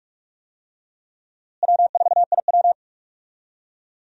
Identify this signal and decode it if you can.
Morse code, heard as a keyed tone, about 46 wpm, 700 Hz. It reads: W4IW
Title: USA Call Signs - 4 Characters - Repeated 2x - 45wpm